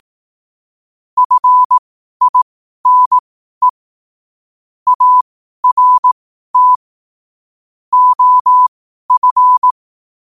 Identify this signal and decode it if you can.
Morse code, heard as a keyed tone, about 18 wpm, spaced as for 12 wpm, 1000 Hz. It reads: FINE ART OF
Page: 12-wpm-morse-code-quotes